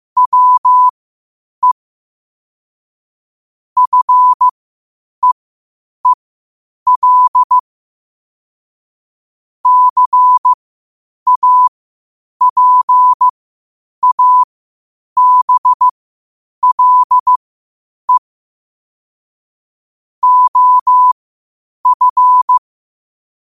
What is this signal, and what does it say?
Morse code, heard as a keyed tone, about 15 wpm, spaced as for 8 wpm, 1000 Hz. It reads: WE FEEL CAPABLE OF
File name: qotdfiles / 8wpm